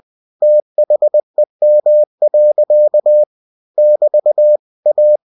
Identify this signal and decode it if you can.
Morse code, heard as a keyed tone, about 20 wpm, 600 Hz. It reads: THEM. <BT> A
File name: cw-books / winnie-the-pooh / winnie-20wpm-page-0019